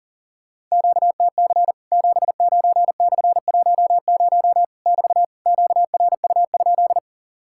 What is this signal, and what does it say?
Morse code, heard as a keyed tone, about 40 wpm, 700 Hz. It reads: QTC 79/10 <BT> QRU?
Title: Worked All Europe DX Contest - QTC Exchange - 40wpm